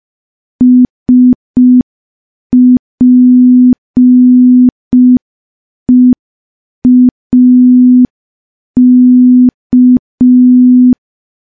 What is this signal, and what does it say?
Morse code, heard as a keyed tone, about 5 wpm, 250 Hz. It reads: SPEAK
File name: morse / morse-training-05WPM